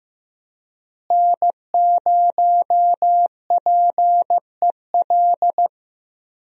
Morse code, heard as a keyed tone, about 15 wpm, 700 Hz. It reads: N0PEL
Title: USA Call Signs - 5 Characters - Repeated 3x - 15wpm